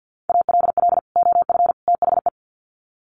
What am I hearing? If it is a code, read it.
Morse code, heard as a keyed tone, about 50 wpm, 700 Hz. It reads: ALL OF THE
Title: Sets of 3 Words 50wpm